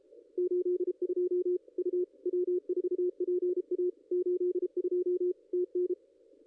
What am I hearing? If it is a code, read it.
Morse code, heard as a keyed tone, about 32 wpm, 360 Hz, with radio noise. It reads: 82 U W4PA 82 TN
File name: contests / ss-cw-k5tr-op-n5rz -2200